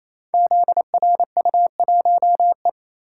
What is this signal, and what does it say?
Morse code, heard as a keyed tone, about 27 wpm, 700 Hz. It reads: ZRU1E
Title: Code Groups - 5 Letter:Numbers - 28wpm